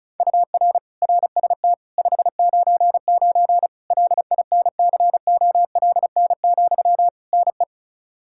Morse code, heard as a keyed tone, about 34 wpm, 700 Hz. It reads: UR RST 599 LINCOLN, NE
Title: Practice QSO - Exchange 1 - Beginning - 35wpm